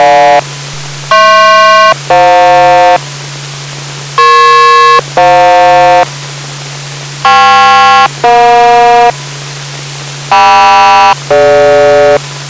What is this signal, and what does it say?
Aircraft SELCAL, heard as two pairs of tones. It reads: HP-FJ EP-FJ KP-EJ KM-DG